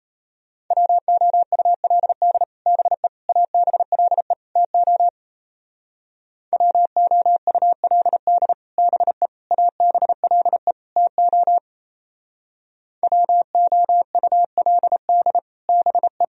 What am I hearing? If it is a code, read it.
Morse code, heard as a keyed tone, about 38 wpm, 700 Hz. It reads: WOULD BE ABLE TO WOULD BE ABLE TO WOULD BE